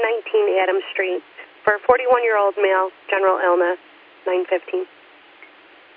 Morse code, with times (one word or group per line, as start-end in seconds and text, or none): none